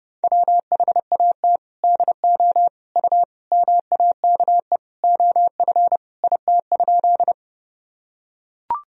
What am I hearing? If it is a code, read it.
Morse code, heard as a keyed tone, about 30 wpm, 700 Hz. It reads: WHAT DO U MAKE OF IT?
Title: Sentences from Top 100 Words and Abbreviations 30wpm